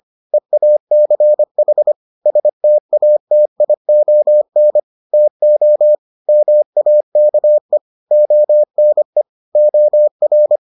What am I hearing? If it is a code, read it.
Morse code, heard as a keyed tone, about 25 wpm, 600 Hz. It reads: EACH STATION TO MAKE ONE OR